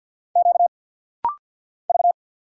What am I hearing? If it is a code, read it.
Morse code, heard as a keyed tone, about 49 wpm, 700 Hz. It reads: Q V
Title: Single Letter-Number Alt - Rapid-Fire 50wpm